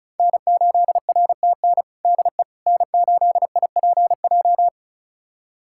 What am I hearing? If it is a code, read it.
Morse code, heard as a keyed tone, about 36 wpm, 700 Hz. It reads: N8RTN DE N8IPJ
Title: USA Callsign DE Callsign - 5 Characters - 35wpm